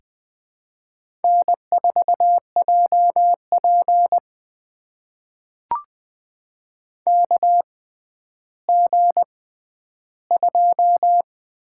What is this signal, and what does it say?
Morse code, heard as a keyed tone, about 20 wpm, spaced as for 11 wpm, 700 Hz. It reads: N4JP KG2
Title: USA Call Signs - 4 Characters - 6x Character Spacing - 20wpm